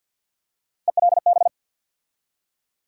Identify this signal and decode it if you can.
Morse code, heard as a keyed tone, about 53 wpm, 700 Hz. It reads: ELB
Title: USA Call Sign Postfixes - 3 Letters 50wpm